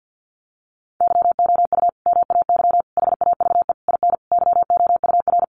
Morse code, heard as a keyed tone, about 50 wpm, 700 Hz. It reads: YOU MAY HAVE IN YOUR
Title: Sentences from Top 1400 Words 50wpm